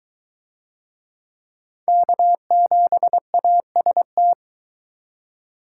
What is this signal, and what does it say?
Morse code, heard as a keyed tone, about 22 wpm, 700 Hz. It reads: K7AST